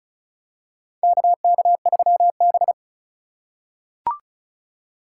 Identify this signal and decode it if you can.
Morse code, heard as a keyed tone, about 36 wpm, 700 Hz. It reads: KK3B
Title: USA Call Signs - 4 Characters - 4x Character Spacing - 35wpm